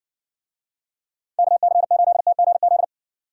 Morse code, heard as a keyed tone, about 60 wpm, 700 Hz. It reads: KC8TGZ